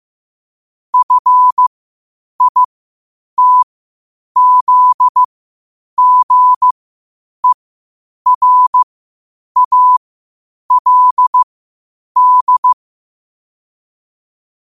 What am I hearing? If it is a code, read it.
Morse code, heard as a keyed tone, about 15 wpm, spaced as for 8 wpm, 1000 Hz. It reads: FITZGERALD
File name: qotdfiles / 8wpm